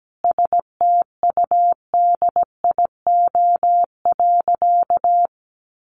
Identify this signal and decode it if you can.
Morse code, heard as a keyed tone, about 17 wpm, 700 Hz. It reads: STUDIO.
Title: Sentences from Top 1200 Words 17wpm